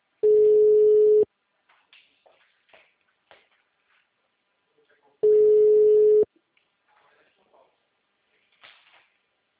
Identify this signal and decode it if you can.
Ringback tone (Europe, 425 Hz), with speech.
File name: out- -2005- - - .2906